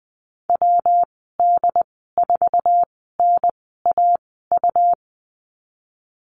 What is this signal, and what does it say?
Morse code, heard as a keyed tone, about 20 wpm, spaced as for 14 wpm, 700 Hz. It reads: WD4NAU
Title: 161 Lesson 26g - Letter g - Call Signs 20wpm